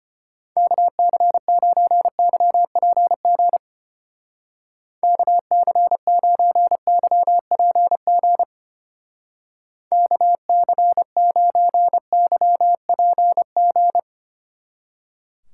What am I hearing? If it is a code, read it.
Morse code, heard as a keyed tone, about 31 wpm, 700 Hz. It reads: KC9YPG KC9YPG KC9YPG
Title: Speed Racing - USA 2x3 Call Signs 25wpm